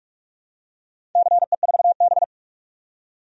Morse code, heard as a keyed tone, about 46 wpm, 700 Hz. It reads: CE4B
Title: DXCC-100 Call Signs - 4 Characters - 45wpm